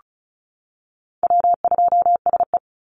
Morse code, heard as a keyed tone, about 34 wpm, 700 Hz. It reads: W2SE
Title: NA SOTA Chaser Callsigns 35wpm